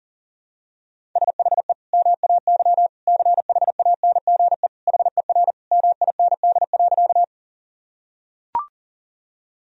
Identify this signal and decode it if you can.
Morse code, heard as a keyed tone, about 37 wpm, 700 Hz. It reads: SHE MAY CHANGE HER MIND.
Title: Sentences from Top 1000 Words - Review Part 1 - 40wpm